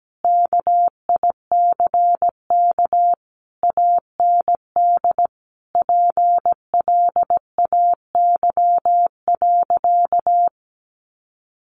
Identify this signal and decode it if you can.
Morse code, heard as a keyed tone, about 17 wpm, 700 Hz. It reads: KICK AND PLAY.
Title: Sentences from Top 1300 Words 17wpm